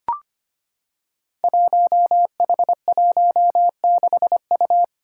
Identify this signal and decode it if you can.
Morse code, heard as a keyed tone, about 25 wpm, 700 Hz. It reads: 1H16U
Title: Five Challenging Letter-Number Combinations 25wpm